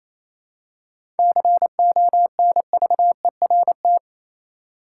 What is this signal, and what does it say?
Morse code, heard as a keyed tone, about 28 wpm, 700 Hz. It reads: CONVERT
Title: Seven Letter Words - 28wpm